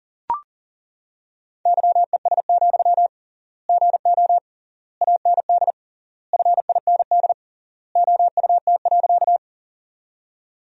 Morse code, heard as a keyed tone, about 40 wpm, 700 Hz. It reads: YES, GO AND FIND OUT.
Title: 3x Word-Spacing - Sentences from Top 500 Words - 40wpm